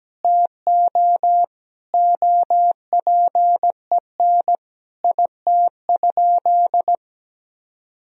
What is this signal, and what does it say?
Morse code, heard as a keyed tone, about 17 wpm, 700 Hz. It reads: TO OPEN IT?
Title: Sentences from Top 1000 Words - Review Part 4 - 17wpm